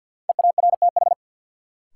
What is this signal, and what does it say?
Morse code, heard as a keyed tone, about 52 wpm, 700 Hz. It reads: EARTH
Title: Sentences from Top 200 Words - New Word Drill - 50wpm